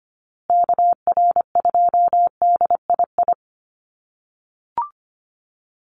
Morse code, heard as a keyed tone, about 25 wpm, 700 Hz. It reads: KR2DII
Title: USA Call Signs Repeated 2x 25wpm